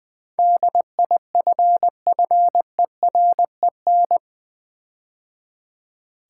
Morse code, heard as a keyed tone, about 20 wpm, 700 Hz. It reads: DIFFEREN
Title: Build A Word - 4 Letters At A Time - 10-Letter Words - 20wpm